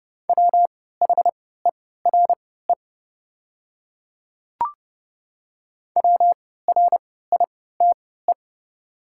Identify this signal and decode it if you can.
Morse code, heard as a keyed tone, about 30 wpm, spaced as for 17 wpm, 690 Hz. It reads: WHERE WRITE
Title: 3x Character Spacing - Top 300 Words - 30wpm